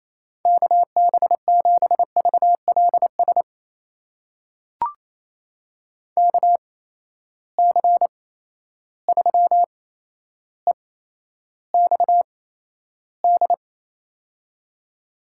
Morse code, heard as a keyed tone, about 28 wpm, spaced as for 12 wpm, 700 Hz. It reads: KB7VLS KC3EXD
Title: USA Call Signs - 6 Characters - 8x Character Spacing - 28wpm